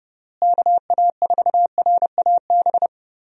Morse code, heard as a keyed tone, about 29 wpm, 700 Hz. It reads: KA4RAB
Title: USA Call Signs - 6 Characters - Repeated 2x - 30wpm